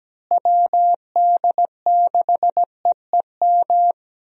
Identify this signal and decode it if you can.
Morse code, heard as a keyed tone, about 17 wpm, 700 Hz. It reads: WD6EEM
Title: USA Call Signs - 6 Characters - Repeated 2x - 17wpm